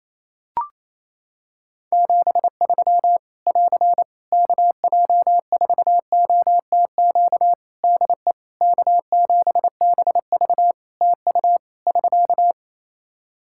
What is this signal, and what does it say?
Morse code, heard as a keyed tone, about 27 wpm, 700 Hz. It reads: 73 <AR> KJ4OTQ DE K7BV TU <SK>
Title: K1USN SST QSO Practice 28wpm